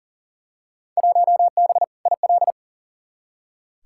Morse code, heard as a keyed tone, about 40 wpm, 700 Hz. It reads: 1B IL
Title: Field Day Exchanges Encore - 40wpm